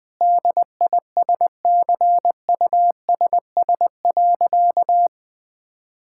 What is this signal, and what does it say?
Morse code, heard as a keyed tone, about 20 wpm, 700 Hz. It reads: DISCUSS.
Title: Sentences with 2-Word Phrases - 20wpm